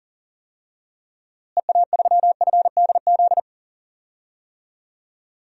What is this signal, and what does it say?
Morse code, heard as a keyed tone, about 40 wpm, 700 Hz. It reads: EA3FDZ
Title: DXCC-100 Call Signs - 6 Characters - Repeated 3x - 40wpm